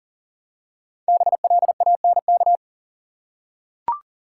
Morse code, heard as a keyed tone, about 40 wpm, 700 Hz. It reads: BLANK
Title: Five Letter Words - 40wpm